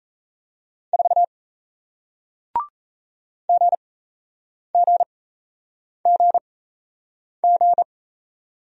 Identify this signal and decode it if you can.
Morse code, heard as a keyed tone, about 37 wpm, spaced as for 8 wpm, 700 Hz. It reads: 4 GGGG